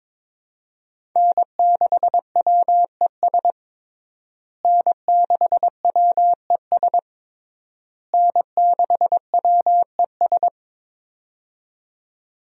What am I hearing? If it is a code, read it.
Morse code, heard as a keyed tone, about 22 wpm, 700 Hz. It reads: N6WES N6WES N6WES
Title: USA Call Signs - 5 Characters - Repeated 3x - 22wpm